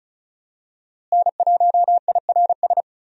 Morse code, heard as a keyed tone, about 36 wpm, 700 Hz. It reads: N1IRS
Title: USA Call Signs - 5 Characters - Repeated 3x - 35wpm